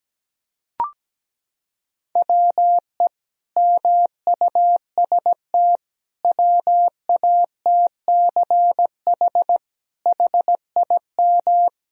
Morse code, heard as a keyed tone, about 17 wpm, 700 Hz. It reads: WE MUST WATCH HIM
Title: Sentences from Top 1000 Words - Review Part 3 - 17wpm